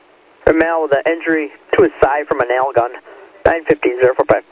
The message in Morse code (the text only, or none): none